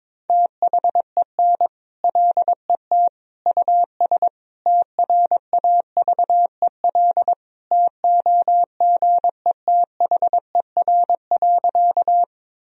Morse code, heard as a keyed tone, about 21 wpm, 700 Hz. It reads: THEN LET US TRAVEL TOGETHER.
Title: Sentences from Top 1000 Words - Review Part 2 - 22wpm